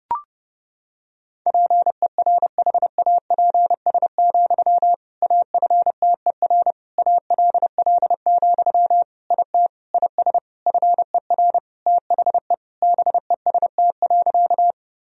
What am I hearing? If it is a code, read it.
Morse code, heard as a keyed tone, about 30 wpm, 700 Hz. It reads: PERHAPS, AFTER ALL, IT IS FER THE BEST.
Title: Sentences from Top 500 Words and Abbreviations 30wpm